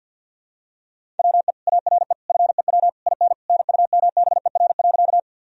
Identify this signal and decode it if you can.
Morse code, heard as a keyed tone, about 52 wpm, 700 Hz. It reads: WE ARE FEW IN NUMBER.
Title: Sentences from Top 1000 Words - Review Part 1 - 50wpm